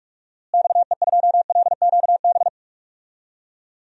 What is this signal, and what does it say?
Morse code, heard as a keyed tone, about 46 wpm, 700 Hz. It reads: XE2LQB
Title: DXCC-100 Call Signs - 6 Characters - Repeated 3x - 45wpm